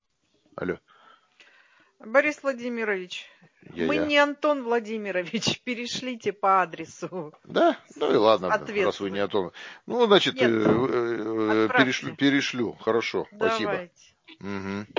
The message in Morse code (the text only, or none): none